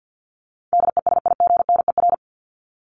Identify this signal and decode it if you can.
Morse code, heard as a keyed tone, about 50 wpm, 700 Hz. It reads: DESIGNER
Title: Top 1501-1750 Words 50wpm